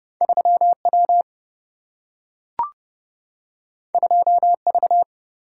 Morse code, heard as a keyed tone, about 30 wpm, 700 Hz. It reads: 3 W 2 V
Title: Single-Letter Single-Number 30wpm